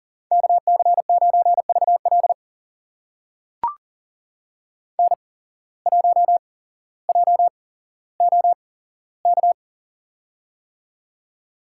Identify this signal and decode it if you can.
Morse code, heard as a keyed tone, about 38 wpm, spaced as for 17 wpm, 700 Hz. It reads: KC9VL N1JOK
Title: USA Call Signs - 5 Characters - 8x Character Spacing - 40wpm